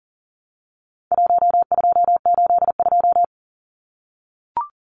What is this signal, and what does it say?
Morse code, heard as a keyed tone, about 38 wpm, 700 Hz. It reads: 1282
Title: Numbers - 4-Digit 40wpm